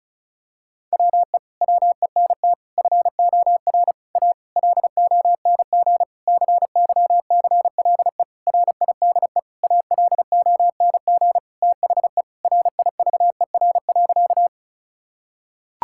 Morse code, heard as a keyed tone, about 36 wpm, 700 Hz. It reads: WE WENT FOR A LONG CYCLE RIDE ALONG THE RIVER.
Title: Sentences from Top 1000 Words - Review Part 6 - 35wpm